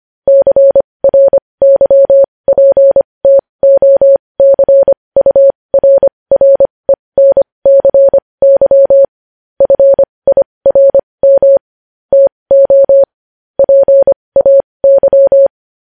Morse code, heard as a keyed tone, about 25 wpm, spaced as for 20 wpm, 550 Hz. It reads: CRYPTOCURRENCY FIRM TO PAY
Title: News Headlines at 20